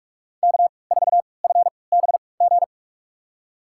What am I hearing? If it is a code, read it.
Morse code, heard as a keyed tone, about 45 wpm, spaced as for 25 wpm, 700 Hz. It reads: K4FBG
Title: USA Call Signs - 5 Characters - 3x Character Spacing - 45wpm